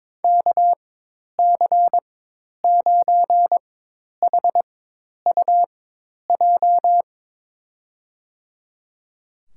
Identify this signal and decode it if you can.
Morse code, heard as a keyed tone, about 22 wpm, spaced as for 10 wpm, 700 Hz. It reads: KC9HUJ
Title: USA Call Signs - 4x Character Spacing - 22wpm